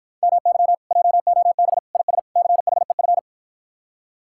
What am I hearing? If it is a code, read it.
Morse code, heard as a keyed tone, about 53 wpm, 700 Hz. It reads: MY JOB IS CHEF